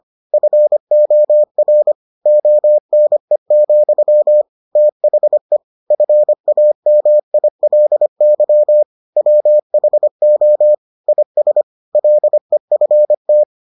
Morse code, heard as a keyed tone, about 25 wpm, 600 Hz. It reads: FOR ONE, THE FAMILY WHO IS LEFT